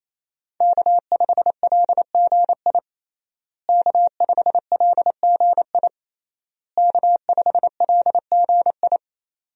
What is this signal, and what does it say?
Morse code, heard as a keyed tone, about 28 wpm, 700 Hz. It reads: K5LGI K5LGI K5LGI
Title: USA Call Signs - 5 Characters - Repeated 3x - 28wpm